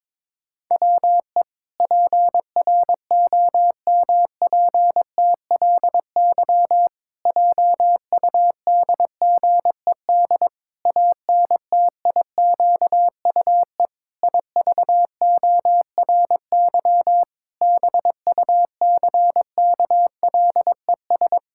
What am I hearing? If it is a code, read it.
Morse code, heard as a keyed tone, about 22 wpm, 700 Hz. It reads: WE PROMPTLY JUDGED ANTIQUE IVORY BUCKLES
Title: Pangram Challenge v2 - 22wpm